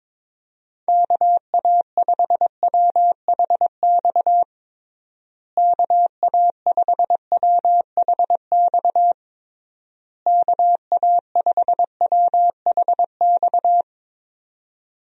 Morse code, heard as a keyed tone, about 23 wpm, 700 Hz. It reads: KA5WHX KA5WHX KA5WHX
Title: USA Call Signs - 6 Characters - Repeated 3x - 22wpm